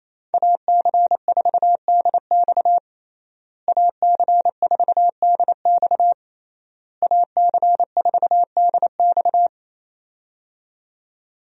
Morse code, heard as a keyed tone, about 27 wpm, 700 Hz. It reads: AC4DX AC4DX AC4DX
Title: USA Call Signs - 5 Characters - Repeated 3x - 28wpm